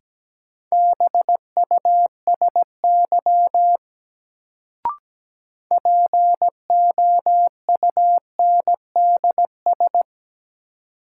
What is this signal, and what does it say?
Morse code, heard as a keyed tone, about 17 wpm, 700 Hz. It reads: BUSY POUNDS
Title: Sentences from Top 1000 Words - Review Part 2 - New Word Drill - 17wpm